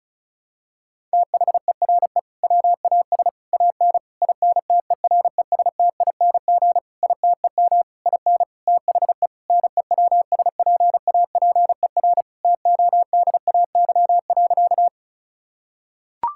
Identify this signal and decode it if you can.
Morse code, heard as a keyed tone, about 34 wpm, 700 Hz. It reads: THERE WAS AN INTERESTING ITEM IN THE NEWSPAPER TODAY.
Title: Sentences from Top 1000 Words - Review Part 6 - 35wpm